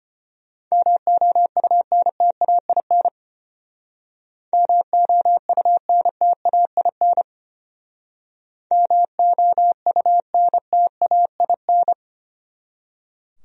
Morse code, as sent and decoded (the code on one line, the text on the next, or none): -- --- ..- -. - .- .. -. / -- --- ..- -. - .- .. -. / -- --- ..- -. - .- .. -.
MOUNTAIN MOUNTAIN MOUNTAIN